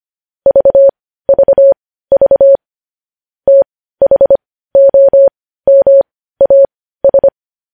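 Morse code, heard as a keyed tone, about 25 wpm, spaced as for 15 wpm, 550 Hz. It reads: VVV THOMAS
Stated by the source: Morse code news headlines sent at 15, preceded by vvv vvv.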